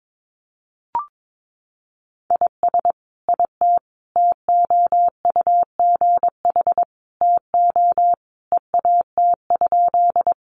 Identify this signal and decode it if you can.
Morse code, heard as a keyed tone, about 22 wpm, 700 Hz. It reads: IS IT TOUGH TO EAT?
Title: Sentences from Top 1200 Words 22wpm